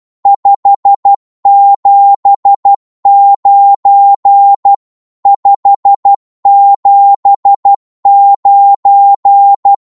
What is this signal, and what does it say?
Morse code, heard as a keyed tone, about 12 wpm, 800 Hz. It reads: 579 579
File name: Morse / qso-12wpm